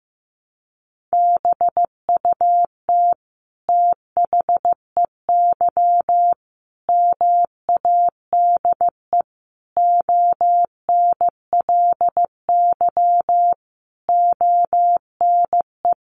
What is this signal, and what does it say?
Morse code, heard as a keyed tone, about 15 wpm, 700 Hz. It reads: BUT THEY MADE ONLY ONE
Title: Sentences from Top 1300 Words 15wpm